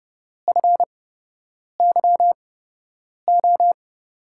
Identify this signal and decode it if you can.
Morse code, heard as a keyed tone, about 31 wpm, spaced as for 8 wpm, 700 Hz. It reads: FYO
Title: USA Call Signs - 6 Characters - 8x Character Spacing - 30wpm